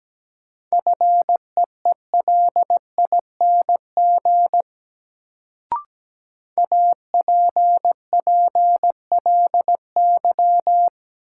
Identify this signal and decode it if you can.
Morse code, heard as a keyed tone, about 17 wpm, 700 Hz. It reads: FEELING APPLY
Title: Sentences from Top 1200 Words - New Word Drill Alt - 17wpm